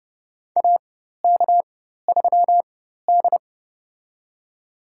Morse code, heard as a keyed tone, about 31 wpm, spaced as for 14 wpm, 700 Hz. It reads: AK3D
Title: USA Call Signs - 4 Characters - 4x Character Spacing - 30wpm